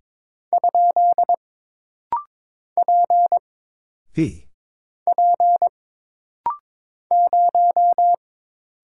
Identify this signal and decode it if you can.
Morse code, heard as a keyed tone, about 22 wpm, 700 Hz. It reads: ? P P 0
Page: This rapid-fire timing is geared to push your ICR (Instant Character Recognition) skills.